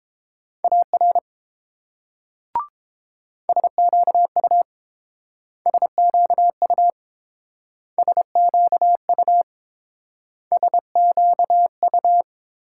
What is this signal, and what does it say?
Morse code, heard as a keyed tone, about 31 wpm, 700 Hz. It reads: AR SQU SQU SQU SQU